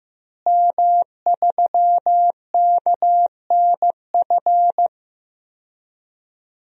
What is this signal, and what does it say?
Morse code, heard as a keyed tone, about 16 wpm, 700 Hz. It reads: M3KNF
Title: DXCC-100 Call Signs - 5 Characters - Repeated 2x - 15wpm